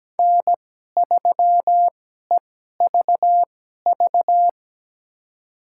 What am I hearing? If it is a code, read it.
Morse code, heard as a keyed tone, about 17 wpm, spaced as for 12 wpm, 700 Hz. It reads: N3EVV
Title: USA Call Signs - 5 Characters - 2x Character Spacing - 17wpm